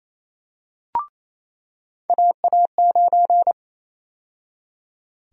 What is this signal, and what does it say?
Morse code, heard as a keyed tone, about 28 wpm, 700 Hz. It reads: AA9
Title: USA Call Sign Prefixes - 28wpm